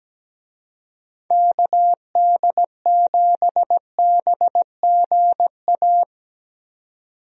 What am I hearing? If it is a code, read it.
Morse code, heard as a keyed tone, about 17 wpm, 700 Hz. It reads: KD7BGA
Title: USA Call Signs - 6 Characters - 17wpm